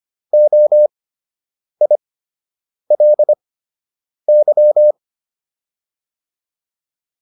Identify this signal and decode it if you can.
Morse code, heard as a keyed tone, about 25 wpm, spaced as for 8 wpm, 600 Hz. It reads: OILY